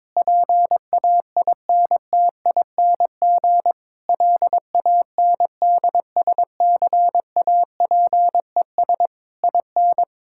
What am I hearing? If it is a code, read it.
Morse code, heard as a keyed tone, about 22 wpm, 700 Hz. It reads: PAINTING LANDSCAPES IN
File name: Difficult Sentences with Hints - 22wpm